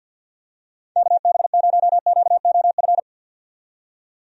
Morse code, heard as a keyed tone, about 53 wpm, 700 Hz. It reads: KB0QOF
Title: USA Call Signs - 6 Characters - 50wpm